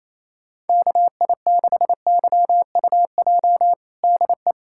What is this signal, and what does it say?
Morse code, heard as a keyed tone, about 28 wpm, 700 Hz. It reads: KI6YUJ DE
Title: USA Callsign DE Callsign 28wpm